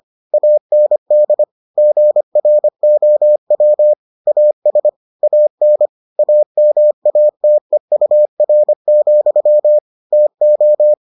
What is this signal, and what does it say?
Morse code, heard as a keyed tone, about 25 wpm, 600 Hz. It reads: AND GROW AS AN AMATEUR, TO